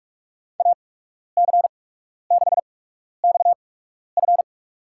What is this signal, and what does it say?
Morse code, heard as a keyed tone, about 46 wpm, spaced as for 12 wpm, 700 Hz. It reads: AC6XF
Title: USA Call Signs - 5 Characters - 8x Character Spacing - 45wpm